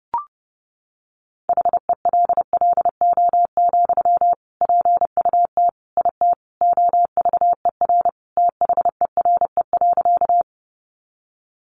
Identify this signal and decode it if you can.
Morse code, heard as a keyed tone, about 28 wpm, 700 Hz. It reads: HELLO, PUT IT OVER THERE.
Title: Sentences from Top 1100 Words 30wpm